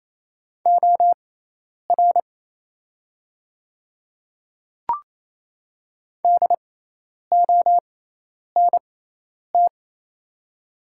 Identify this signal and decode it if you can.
Morse code, heard as a keyed tone, about 28 wpm, spaced as for 10 wpm, 700 Hz. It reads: OR DONT
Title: 6x Character Spacing - Top 300 Words - 28wpm